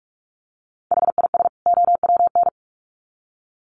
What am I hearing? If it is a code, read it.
Morse code, heard as a keyed tone, about 44 wpm, 700 Hz. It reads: HIS OWN
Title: Sets of 2 Words 45wpm